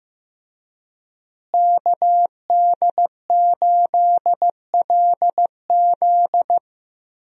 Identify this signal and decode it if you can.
Morse code, heard as a keyed tone, about 15 wpm, 700 Hz. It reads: KD8LZ
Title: USA Call Signs - 5 Characters - Repeated 3x - 15wpm